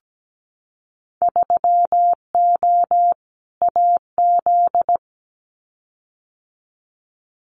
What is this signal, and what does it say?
Morse code, heard as a keyed tone, about 17 wpm, 700 Hz. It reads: 3O AZ
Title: Winter Field Day Exchanges 17wpm